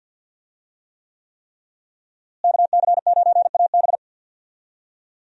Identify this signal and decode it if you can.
Morse code, heard as a keyed tone, about 52 wpm, 700 Hz. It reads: KC9AB
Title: USA Call Signs - 5 Characters - Repeated 3x - 50wpm